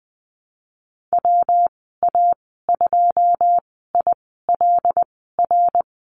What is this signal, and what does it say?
Morse code, heard as a keyed tone, about 20 wpm, spaced as for 14 wpm, 700 Hz. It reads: WA2ILR
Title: 122 Lesson 20g - Number 3 - Call Signs 20wpm